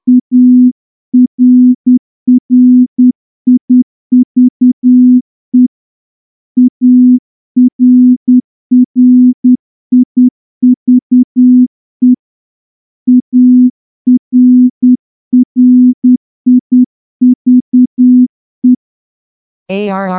Morse code, heard as a keyed tone, about 10 wpm, 250 Hz. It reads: ARRIVE ARRIVE ARRIVE
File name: morse-training-10WPM